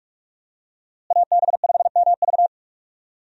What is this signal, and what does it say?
Morse code, heard as a keyed tone, about 46 wpm, 700 Hz. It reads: AB5MV
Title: USA Call Signs - 5 Characters - Repeated 3x - 45wpm